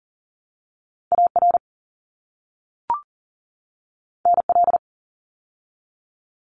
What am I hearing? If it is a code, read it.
Morse code, heard as a keyed tone, about 40 wpm, 700 Hz. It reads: AR NL
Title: ARRL Sections 40wpm